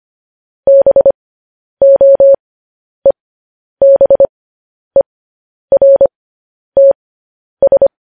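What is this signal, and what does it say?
Morse code, heard as a keyed tone, about 25 wpm, spaced as for 10 wpm, 550 Hz. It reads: BOEBERTS
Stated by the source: Morse news headlines at 10.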